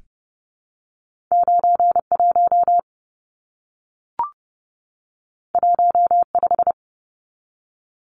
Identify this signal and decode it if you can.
Morse code, heard as a keyed tone, about 30 wpm, 700 Hz. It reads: 91 15
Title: Numbers - 2-Digit 30wpm